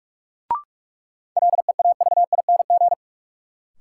Morse code, heard as a keyed tone, about 46 wpm, 700 Hz. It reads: LEAVING
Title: Sentences from Top 1000 Words - Review Part 4 - New Word Drill - 45wpm